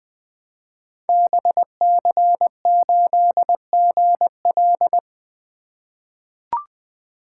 Morse code, heard as a keyed tone, about 20 wpm, 700 Hz. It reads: BC8GL
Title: Code Groups - 5 Letter:Numbers - 20wpm